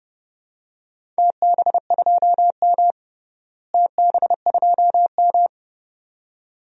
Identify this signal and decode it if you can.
Morse code, heard as a keyed tone, about 32 wpm, 700 Hz. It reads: TB2M TB2M
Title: DXCC-100 Call Signs - 4 Characters - Repeated 2x - 30wpm